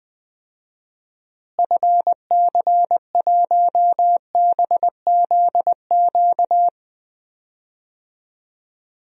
Morse code, heard as a keyed tone, about 20 wpm, 700 Hz. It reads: FC1BZQ
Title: DXCC-100 Call Signs - 6 Characters - Repeated 2x - 20wpm